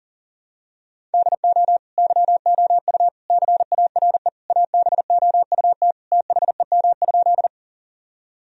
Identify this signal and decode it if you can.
Morse code, heard as a keyed tone, about 40 wpm, 700 Hz. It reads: DO YOU CARE ABOUT THEM?
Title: Sentences from Top 1000 Words - Review Part 1 - 40wpm